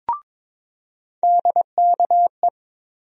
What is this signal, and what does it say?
Morse code, heard as a keyed tone, about 22 wpm, 700 Hz. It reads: DKE
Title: USA Call Sign Postfixes - 3 Letters 22wpm